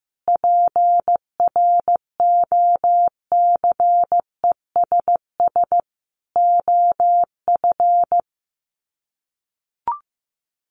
Morse code, heard as a keyed tone, about 15 wpm, 700 Hz. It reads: PROCESS OF
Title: Sets of 5 Words Encore2 15wpm